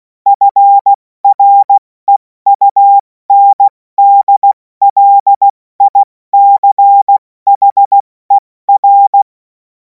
Morse code, heard as a keyed tone, about 16 wpm, spaced as for 14 wpm, 800 Hz. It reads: FREUNDLICHER
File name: Morsekurs / Witze / Witz 05. Mai 2026 70 F80